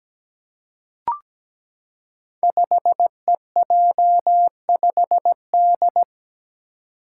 Morse code, heard as a keyed tone, about 17 wpm, 700 Hz. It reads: 5EJ5D
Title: Five Challenging Letter-Number Combinations 17wpm